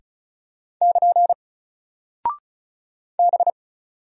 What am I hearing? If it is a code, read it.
Morse code, heard as a keyed tone, about 36 wpm, 700 Hz. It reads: <KN> B
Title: Single Character (Letter:Number:Punctuation:Prosign) Alt 35wpm